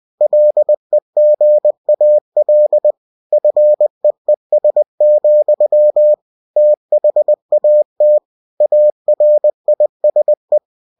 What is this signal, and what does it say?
Morse code, heard as a keyed tone, about 20 wpm, 600 Hz. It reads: LEGAL FEES, THAT ARISE
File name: cw-books / winnie-the-pooh / winnie-20wpm-page-0120